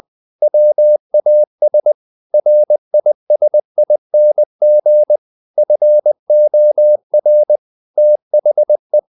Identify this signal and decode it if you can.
Morse code, heard as a keyed tone, about 20 wpm, 600 Hz. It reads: WAS RISING FOR THE
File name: the old man and the sea part 2 20wpm